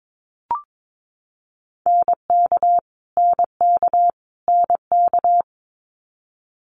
Morse code, heard as a keyed tone, about 22 wpm, 700 Hz. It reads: NK NK NK
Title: Multi-letter Phonemes Repeated 3x 22wpm